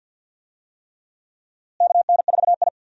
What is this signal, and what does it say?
Morse code, heard as a keyed tone, about 46 wpm, 700 Hz. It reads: KN4I
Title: USA Call Signs - 4 Characters - Repeated 3x - 50wpm